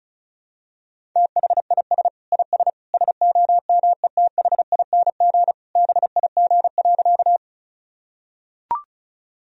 Morse code, heard as a keyed tone, about 36 wpm, 700 Hz. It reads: THIS IS SOMETHING BIG.
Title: Sentences from Top 1000 Words - Review Part 1 - 35wpm